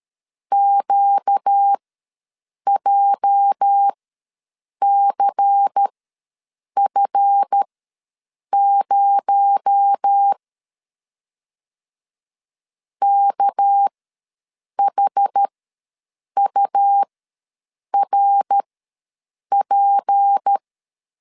Morse code, heard as a keyed tone, about 13 wpm, spaced as for 7 wpm, 800 Hz. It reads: QJCF0 KHURP
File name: LEZIONI / 8 YJ45 / 8 YJ ws 12wpm 3342